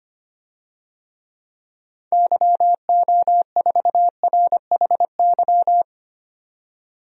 Morse code, heard as a keyed tone, about 25 wpm, 700 Hz. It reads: YO4RHY
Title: DXCC-100 Call Signs - 6 Characters - 25wpm